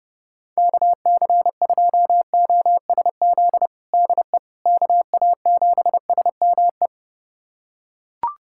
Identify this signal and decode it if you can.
Morse code, heard as a keyed tone, about 32 wpm, 700 Hz. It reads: KC2OSZ DE KA7SME
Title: USA Callsign DE Callsign - 6 Characters - 30wpm